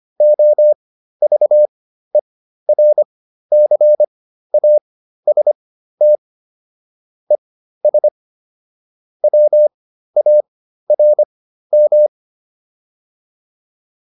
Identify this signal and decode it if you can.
Morse code, heard as a keyed tone, about 25 wpm, spaced as for 13 wpm, 600 Hz. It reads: OVERCAST ES WARM